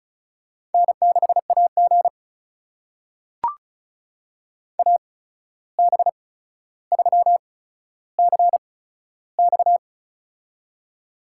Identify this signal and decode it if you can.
Morse code, heard as a keyed tone, about 36 wpm, spaced as for 14 wpm, 700 Hz. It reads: N6AG AB3CX
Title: USA Call Signs - 8x Character Spacing - 35wpm